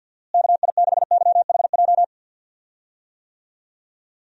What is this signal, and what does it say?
Morse code, heard as a keyed tone, about 51 wpm, 700 Hz. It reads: KI6YHJ
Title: USA Call Signs - 6 Characters - Repeated 2x - 50wpm